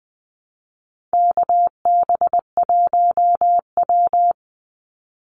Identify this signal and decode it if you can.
Morse code, heard as a keyed tone, about 20 wpm, 700 Hz. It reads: KB1W
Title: Call Signs Repeated 3x - 20wpm